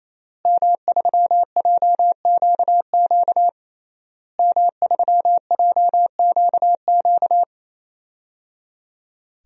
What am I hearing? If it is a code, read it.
Morse code, heard as a keyed tone, about 27 wpm, 700 Hz. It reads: M3JQQ M3JQQ
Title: DXCC-100 Call Signs - 5 Characters - Repeated 3x - 28wpm